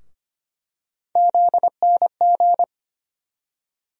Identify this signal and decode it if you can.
Morse code, heard as a keyed tone, about 25 wpm, 700 Hz. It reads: ZNG
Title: USA Call Sign Postfixes - 3 Letters 25wpm